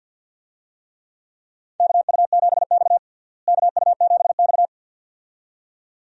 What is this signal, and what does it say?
Morse code, heard as a keyed tone, about 52 wpm, 700 Hz. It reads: KU7X KU7X
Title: USA Call Signs - 4 Characters - Repeated 2x - 50wpm